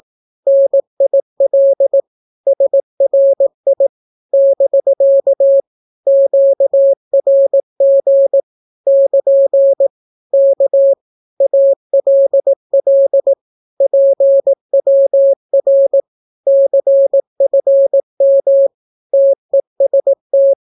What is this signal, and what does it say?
Morse code, heard as a keyed tone, about 18 wpm, 550 Hz. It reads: NIL SRI <BK> QRG <KN> K ALL PWR CFM TEST
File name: naukapodstaw / skroty / skroty